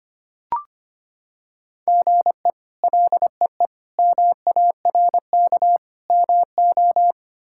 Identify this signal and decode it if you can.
Morse code, heard as a keyed tone, about 25 wpm, 700 Hz. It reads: GE LEE MARK MO
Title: K1USN SST QSO Practice - 25wpm